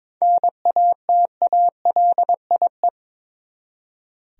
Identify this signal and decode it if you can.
Morse code, heard as a keyed tone, about 22 wpm, 700 Hz. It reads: NATALIE
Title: USA Common Names - 22wpm